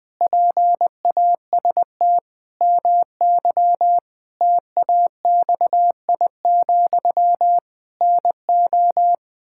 Morse code, heard as a keyed tone, about 20 wpm, 700 Hz. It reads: PAST MY TAXI, NO
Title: Pangram Challenge v2 - 20wpm